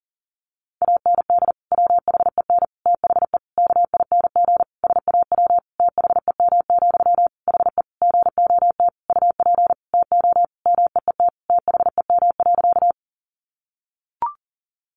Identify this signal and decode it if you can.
Morse code, heard as a keyed tone, about 40 wpm, 700 Hz. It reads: AND WHEN THE KING SAW THEM, HE GOT UP TO MEET THEM.
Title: Sentences from Top 1100 Words 40wpm